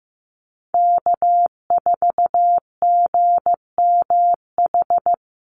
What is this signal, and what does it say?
Morse code, heard as a keyed tone, about 15 wpm, 700 Hz. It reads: K4GMH
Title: Call Signs Repeated 2x - 15wpm